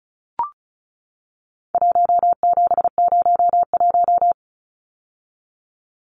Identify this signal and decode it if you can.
Morse code, heard as a keyed tone, about 34 wpm, 700 Hz. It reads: 1701
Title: Numbers - 4-Digit 35wpm